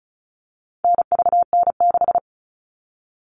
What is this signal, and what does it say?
Morse code, heard as a keyed tone, about 34 wpm, 700 Hz. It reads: NVN6
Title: All Cut Numbers - 4-Digit 35wpm